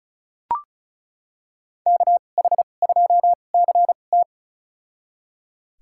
Morse code, heard as a keyed tone, about 36 wpm, spaced as for 25 wpm, 700 Hz. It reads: KH2CT
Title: USA Call Signs - 5 Characters - 2x Character Spacing - 35wpm